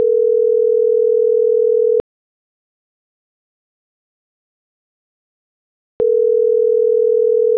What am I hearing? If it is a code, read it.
Ringback tone (North America), 440 + 480 Hz.